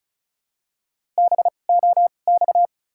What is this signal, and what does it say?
Morse code, heard as a keyed tone, about 36 wpm, spaced as for 25 wpm, 700 Hz. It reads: BOX
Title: 2x Character Spacing - Top 500 Words - 35wpm